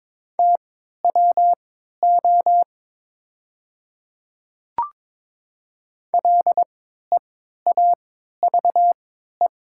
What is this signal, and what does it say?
Morse code, heard as a keyed tone, about 22 wpm, spaced as for 12 wpm, 700 Hz. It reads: TWO LEAVE
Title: 3x Character Spacing - Top 500 Words - 22wpm